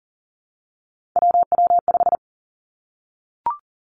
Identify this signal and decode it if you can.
Morse code, heard as a keyed tone, about 38 wpm, 700 Hz. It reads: WW5
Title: Three Challenging Letter-Number Combinations 40wpm